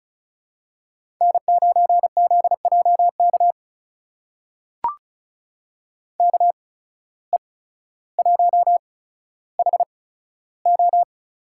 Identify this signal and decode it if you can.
Morse code, heard as a keyed tone, about 36 wpm, spaced as for 15 wpm, 700 Hz. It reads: N9ZJK KE1HO
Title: USA Call Signs - 5 Characters - 8x Character Spacing - 35wpm